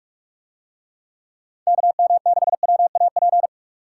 Morse code, heard as a keyed tone, about 42 wpm, 700 Hz. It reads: KM6WAP